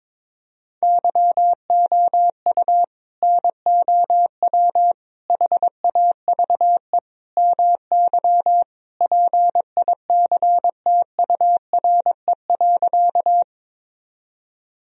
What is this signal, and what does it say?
Morse code, heard as a keyed tone, about 22 wpm, 700 Hz. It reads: YOU NOW HAVE MY PICTURE.
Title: Sentences from Top 1000 Words - Review Part 1 - 22wpm